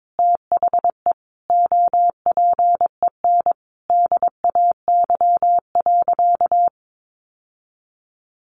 Morse code, heard as a keyed tone, about 22 wpm, 700 Hz. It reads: THE OPEN DAY.
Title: Sentences from Top 1500 Words 22wpm